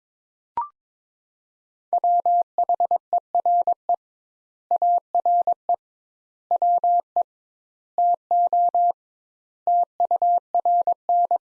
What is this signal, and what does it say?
Morse code, heard as a keyed tone, about 22 wpm, 700 Hz. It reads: WHERE ARE WE TO TURN
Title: Sentences Top 200 Words - 22wpm